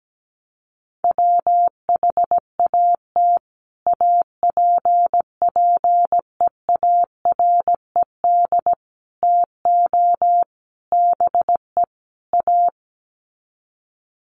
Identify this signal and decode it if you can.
Morse code, heard as a keyed tone, about 17 wpm, 700 Hz. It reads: WHAT APPEARED TO BE A
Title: Sets of 5 Words Encore2 17wpm